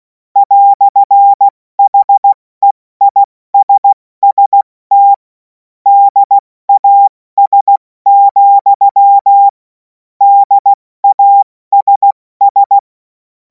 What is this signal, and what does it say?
Morse code, heard as a keyed tone, about 16 wpm, 800 Hz. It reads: "HEISST DAS, DASS
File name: Morsekurs / Witze / Witz 07. Juli 2026 70 F80